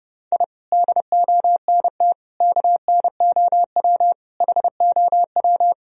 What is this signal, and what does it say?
Morse code, heard as a keyed tone, about 32 wpm, 700 Hz. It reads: I DONT KNOW HOW
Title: Sentences from Top 1000 Words - Review Part 1 - 30wpm